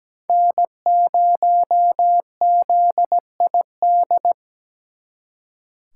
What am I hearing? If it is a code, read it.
Morse code, heard as a keyed tone, about 20 wpm, 700 Hz. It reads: N0ZID
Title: Speed Racing - USA 1x3 Call Signs 17wpm